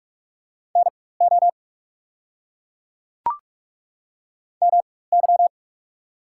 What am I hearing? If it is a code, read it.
Morse code, heard as a keyed tone, about 42 wpm, spaced as for 21 wpm, 690 Hz. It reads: NO MY
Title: 4x Character Spacing - Top 300 Words - 45wpm